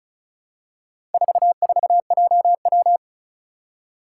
Morse code, heard as a keyed tone, about 36 wpm, 700 Hz. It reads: 44JW
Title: Four Challenging Letter-Number Combinations 35wpm